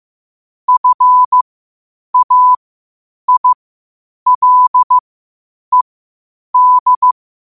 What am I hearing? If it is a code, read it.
Morse code, heard as a keyed tone, about 15 wpm, spaced as for 8 wpm, 1000 Hz. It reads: FAILED
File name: qotdfiles / 8wpm